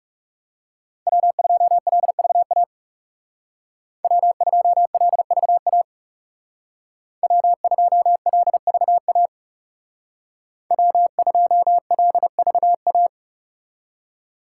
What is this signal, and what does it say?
Morse code, heard as a keyed tone, about 36 wpm, 700 Hz. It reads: W2LVA W2LVA W2LVA W2LVA
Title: Speed Racing - USA 1x3 Call Signs 30wpm